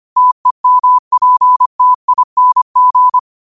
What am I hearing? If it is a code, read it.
Morse code, heard as a keyed tone, about 25 wpm, 1000 Hz. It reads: TEMPTING
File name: qotdfiles / 25wpm